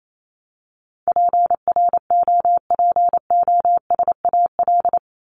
Morse code, heard as a keyed tone, about 28 wpm, 700 Hz. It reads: PROPOSAL
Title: Top 1751-2000 Words 28wpm